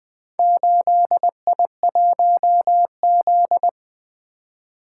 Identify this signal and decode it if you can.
Morse code, heard as a keyed tone, about 20 wpm, 700 Hz. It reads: 8I1Z
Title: Four Letter-Number Combinations - 20wpm